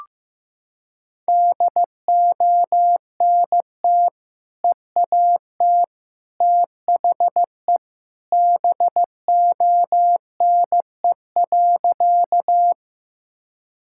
Morse code, heard as a keyed tone, about 15 wpm, 700 Hz. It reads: DONT EAT THE BONE.
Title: Sentences from Top 1600 Words 15wpm